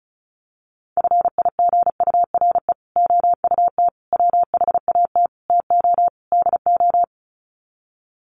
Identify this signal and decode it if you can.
Morse code, heard as a keyed tone, about 34 wpm, 700 Hz. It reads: FIGURE OUT WHAT TO DO
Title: Sets of 5 Words Encore2 35wpm